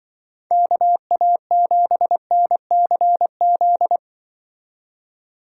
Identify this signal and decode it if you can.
Morse code, heard as a keyed tone, about 25 wpm, 700 Hz. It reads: KA7NCZ